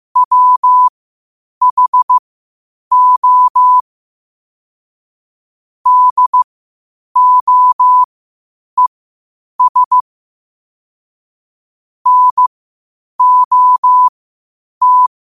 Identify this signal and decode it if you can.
Morse code, heard as a keyed tone, about 15 wpm, spaced as for 8 wpm, 1000 Hz. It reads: WHO DOES NOT
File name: qotdfiles / 8wpm